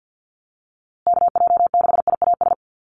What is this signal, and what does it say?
Morse code, heard as a keyed tone, about 50 wpm, 700 Hz. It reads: KJ6IAS
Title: USA Call Signs Repeated 2x 50wpm